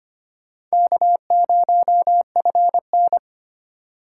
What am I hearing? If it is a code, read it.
Morse code, heard as a keyed tone, about 25 wpm, 700 Hz. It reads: K0FN
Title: USA Call Signs - 4 Characters - 4x Character Spacing - 25wpm